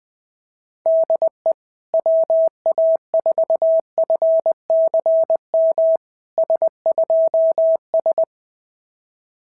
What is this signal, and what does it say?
Morse code, heard as a keyed tone, about 20 wpm, 650 Hz. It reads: DE WA4FCM S2S
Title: SOTA QSO Practice 20wpm